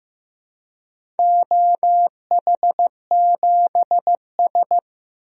Morse code, heard as a keyed tone, about 15 wpm, 700 Hz. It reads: OH7S
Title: DXCC-100 Call Signs - 4 Characters - 15wpm